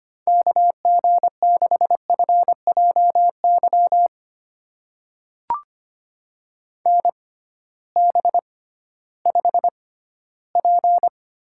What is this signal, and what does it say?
Morse code, heard as a keyed tone, about 25 wpm, spaced as for 15 wpm, 700 Hz. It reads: KG6FJY NB5P
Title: USA Call Signs - 6x Character Spacing - 25wpm